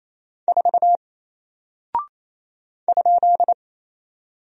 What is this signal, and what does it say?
Morse code, heard as a keyed tone, about 28 wpm, 700 Hz. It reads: 4 ?
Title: Single Character (Letter:Number:Punctuation:Prosign) 28wpm